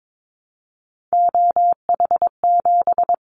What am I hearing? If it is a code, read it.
Morse code, heard as a keyed tone, about 22 wpm, 700 Hz. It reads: OH7
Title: Callsign Prefixes for Top 25 DXCC - Repeated 3x - 22wpm